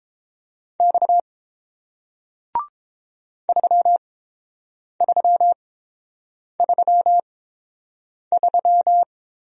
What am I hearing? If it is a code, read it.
Morse code, heard as a keyed tone, about 32 wpm, spaced as for 7 wpm, 700 Hz. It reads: X 3333